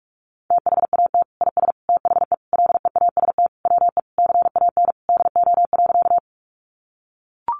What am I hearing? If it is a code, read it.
Morse code, heard as a keyed tone, about 44 wpm, 700 Hz. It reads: THAT IS THE LEAST WE CAN DO.
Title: Sentences from Top 700 Words 45wpm